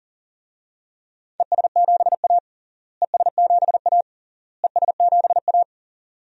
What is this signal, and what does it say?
Morse code, heard as a keyed tone, about 40 wpm, 700 Hz. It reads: ES7A ES7A ES7A
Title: DXCC-100 Call Signs - 4 Characters - Repeated 3x - 40wpm